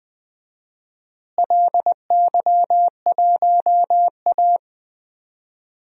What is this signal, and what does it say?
Morse code, heard as a keyed tone, about 20 wpm, 700 Hz. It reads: LY1A
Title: DXCC-100 Call Signs - 4 Characters - Repeated 3x - 20wpm